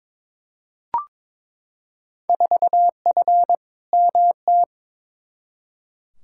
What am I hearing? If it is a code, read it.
Morse code, heard as a keyed tone, about 22 wpm, 700 Hz. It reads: 4F MT
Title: Field Day Exchanges Encore - 22wpm